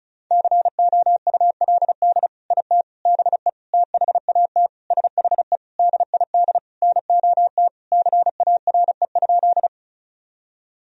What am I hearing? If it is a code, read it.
Morse code, heard as a keyed tone, about 36 wpm, 700 Hz. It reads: COULD IT BE THAT SHE DID NOT CARE?
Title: Sentences from Top 1000 Words - Review Part 1 - 35wpm